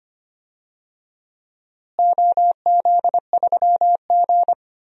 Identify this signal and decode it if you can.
Morse code, heard as a keyed tone, about 25 wpm, 700 Hz. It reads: OZ3G
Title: DXCC-100 Call Signs - 4 Characters - Repeated 2x - 25wpm